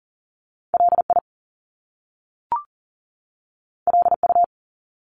Morse code, heard as a keyed tone, about 39 wpm, 700 Hz. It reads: LI LU
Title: Top Two Letter Combinations in Words 40wpm